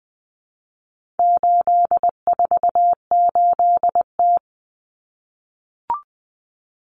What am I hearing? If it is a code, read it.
Morse code, heard as a keyed tone, about 20 wpm, 700 Hz. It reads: 848T
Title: Common Cut Numbers - 4-Digit 20wpm